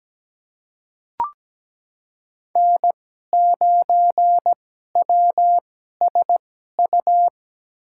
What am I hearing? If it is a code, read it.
Morse code, heard as a keyed tone, about 17 wpm, spaced as for 12 wpm, 700 Hz. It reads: N9WSU
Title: USA Call Signs - 5 Characters - 2x Character Spacing - 17wpm